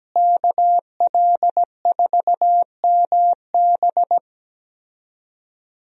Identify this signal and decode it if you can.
Morse code, heard as a keyed tone, about 17 wpm, 700 Hz. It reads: KL4MB
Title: USA Call Signs - 5 Characters - Repeated 2x - 17wpm